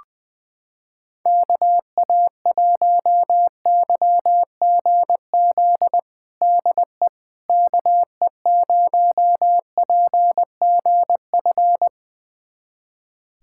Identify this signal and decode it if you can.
Morse code, heard as a keyed tone, about 20 wpm, 700 Hz. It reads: KA1YGZ DE KE0PGF
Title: USA Callsign DE Callsign - 6 Characters - 20wpm